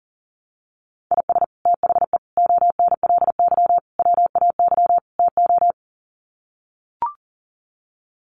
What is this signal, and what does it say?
Morse code, heard as a keyed tone, about 40 wpm, 700 Hz. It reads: IS THE ONLY WAY TO
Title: Sets of 5 Words Encore 40wpm